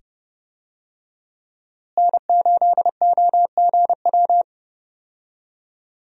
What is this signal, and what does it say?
Morse code, heard as a keyed tone, about 30 wpm, 700 Hz. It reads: N8OGW
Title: USA Call Signs - 5 Characters - Repeated 2x - 30wpm